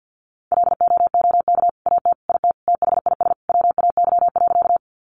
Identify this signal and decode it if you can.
Morse code, heard as a keyed tone, about 51 wpm, 700 Hz. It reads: LOOK AT IT THIS WAY.
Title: Sentences from Top 100 Words 50wpm